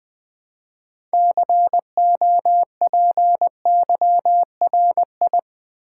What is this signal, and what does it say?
Morse code, heard as a keyed tone, about 20 wpm, 700 Hz. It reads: COPYRI
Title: Build A Word - 3 Letters At A Time - 9-Letter Words - 20wpm